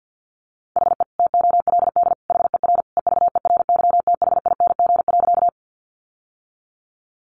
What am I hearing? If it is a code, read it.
Morse code, heard as a keyed tone, about 50 wpm, 700 Hz. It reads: HE TOLD HER EVERYTHING.
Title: Sentences from Top 700 Words 50wpm